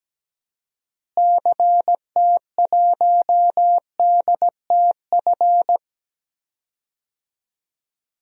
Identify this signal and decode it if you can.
Morse code, heard as a keyed tone, about 17 wpm, 700 Hz. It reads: CT1DTF
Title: DXCC-100 Call Signs - 6 Characters - Repeated 2x - 17wpm